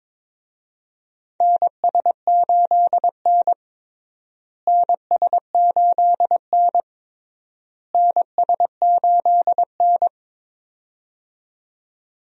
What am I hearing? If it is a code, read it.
Morse code, heard as a keyed tone, about 22 wpm, 700 Hz. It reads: NS8N NS8N NS8N
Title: USA Call Signs - 4 Characters - Repeated 3x - 22wpm